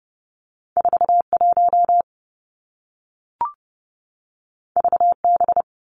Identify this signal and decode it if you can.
Morse code, heard as a keyed tone, about 30 wpm, 700 Hz. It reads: 41 VB
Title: Two Challenging Letter-Number Combinations 30wpm